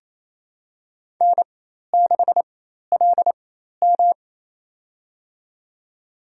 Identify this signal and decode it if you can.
Morse code, heard as a keyed tone, about 28 wpm, spaced as for 13 wpm, 700 Hz. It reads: N6LM
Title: USA Call Signs - 4x Character Spacing - 28wpm